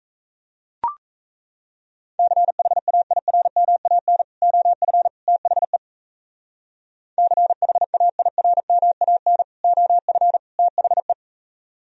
Morse code, heard as a keyed tone, about 40 wpm, 700 Hz. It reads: CHAIRMAN OF THE CHAIRMAN OF THE